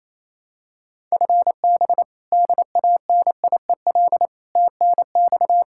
Morse code, heard as a keyed tone, about 28 wpm, 700 Hz. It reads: FB DANIEL TNX
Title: Practice QSO - Exchange 3 - More Info - 28wpm